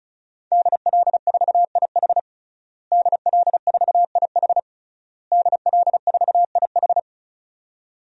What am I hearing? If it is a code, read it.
Morse code, heard as a keyed tone, about 36 wpm, 700 Hz. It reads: DL4IH DL4IH DL4IH
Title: DXCC-100 Call Signs - 5 Characters - Repeated 3x - 35wpm